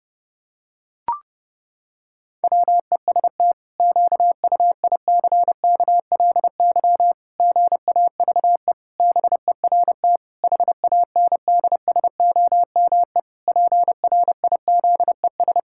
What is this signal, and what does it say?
Morse code, heard as a keyed tone, about 30 wpm, 700 Hz. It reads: WEST QUICKLY GAVE BERT HANDSOME PRIZES
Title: Pangram Challenge v2 - 30wpm